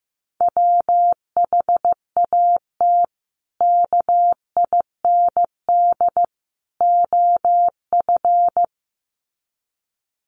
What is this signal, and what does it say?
Morse code, heard as a keyed tone, about 15 wpm, 700 Hz. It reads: WHAT KIND OF
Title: Sets of 3 Words 15wpm